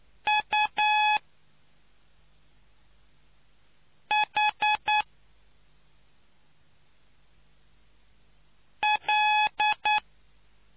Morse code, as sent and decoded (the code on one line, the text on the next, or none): ..- .... .-..
UHL